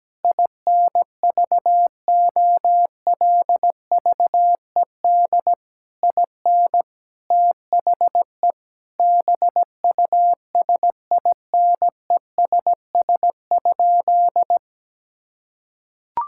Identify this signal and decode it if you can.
Morse code, heard as a keyed tone, about 17 wpm, 700 Hz. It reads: INVOLVED IN THE BUSINESS?
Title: Sentences from Top 1000 Words - Review Part 5 - 17wpm